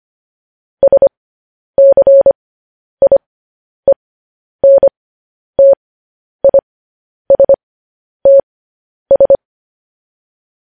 Morse code, heard as a keyed tone, about 25 wpm, spaced as for 10 wpm, 550 Hz. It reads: SCIENTISTS
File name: podcast / News headlines-1x-25x10